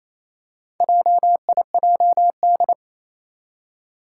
Morse code, heard as a keyed tone, about 27 wpm, 700 Hz. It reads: JIJD
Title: Four Challenging Letter-Number Combinations 28wpm